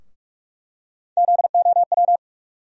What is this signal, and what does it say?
Morse code, heard as a keyed tone, about 46 wpm, 700 Hz. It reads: ZOW
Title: USA Call Sign Postfixes - 3 Letters 45wpm